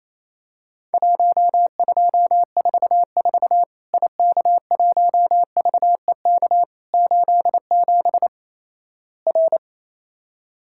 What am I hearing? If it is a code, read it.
Morse code, heard as a keyed tone, about 28 wpm, 700 Hz. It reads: 1244 IK1VEK 87 R
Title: Worked All Europe DX Contest - QTC Exchange - 28wpm